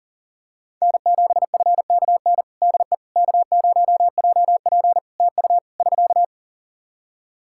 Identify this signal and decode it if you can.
Morse code, heard as a keyed tone, about 37 wpm, 700 Hz. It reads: N7FKN DE K0JP TU <SK>
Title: Practice QSO - Exchange 4 - Ending - 40wpm